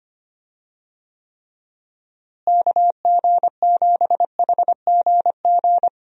Morse code, heard as a keyed tone, about 25 wpm, 700 Hz. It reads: KG7HGG
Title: USA Call Signs - 6 Characters - Repeated 3x - 25wpm